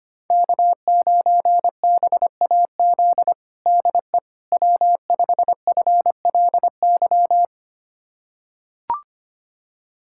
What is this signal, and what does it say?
Morse code, heard as a keyed tone, about 25 wpm, 700 Hz. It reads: K9BAZ DE W5FLY
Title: USA Callsign DE Callsign - 5 Characters - 25wpm